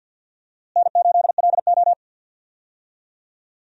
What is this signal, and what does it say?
Morse code, heard as a keyed tone, about 52 wpm, 700 Hz. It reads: N8LO
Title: USA Call Signs - 4 Characters - Repeated 2x - 50wpm